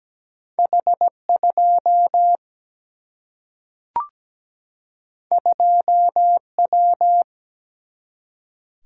Morse code, heard as a keyed tone, about 17 wpm, 700 Hz. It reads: H2 2W
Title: Single-Letter Single-Number 17wpm